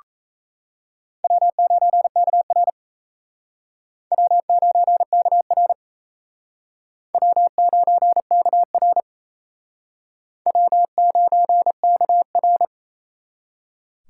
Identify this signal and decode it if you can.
Morse code, heard as a keyed tone, about 36 wpm, 700 Hz. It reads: W9KR W9KR W9KR W9KR